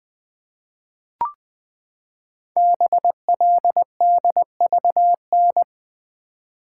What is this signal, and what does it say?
Morse code, heard as a keyed tone, about 20 wpm, 700 Hz. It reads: BLDVN
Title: Code Groups - 5 Letter:Numbers - 20wpm